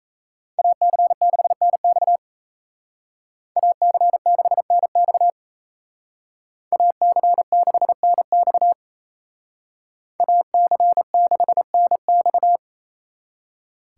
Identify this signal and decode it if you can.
Morse code, heard as a keyed tone, about 35 wpm, 700 Hz. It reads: AC6NX AC6NX AC6NX AC6NX